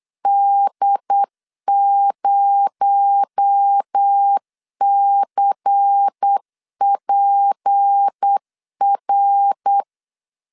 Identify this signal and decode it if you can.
Morse code, heard as a keyed tone, about 8 wpm, 800 Hz. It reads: D0CPR
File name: 8wpm